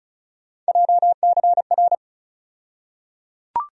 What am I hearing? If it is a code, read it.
Morse code, heard as a keyed tone, about 34 wpm, 700 Hz. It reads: JCR
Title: USA Call Sign Postfixes - 3 Letters 35wpm